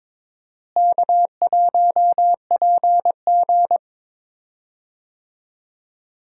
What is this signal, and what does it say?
Morse code, heard as a keyed tone, about 22 wpm, 700 Hz. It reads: K1PG
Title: USA Call Signs - 4 Characters - Repeated 3x - 22wpm